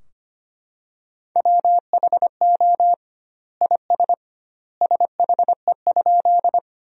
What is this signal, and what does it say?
Morse code, heard as a keyed tone, about 25 wpm, 700 Hz. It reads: WHO IS SHE?
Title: 2x Word-Spacing - Sentences from Top 100 Words - 25wpm